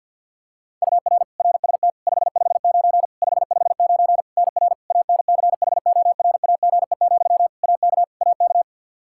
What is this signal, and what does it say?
Morse code, heard as a keyed tone, about 53 wpm, 700 Hz. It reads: UR RST 559 559 NR ANCHORAGE, AK AK
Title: Practice QSO - Exchange 1 - Beginning - 50wpm